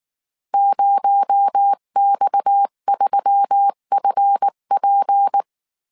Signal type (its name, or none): Morse code